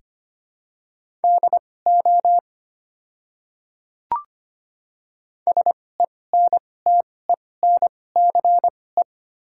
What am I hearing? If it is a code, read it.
Morse code, heard as a keyed tone, about 25 wpm, spaced as for 18 wpm, 690 Hz. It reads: DO SENTENCE
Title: 2x Character Spacing - Top 200 Words - 25wpm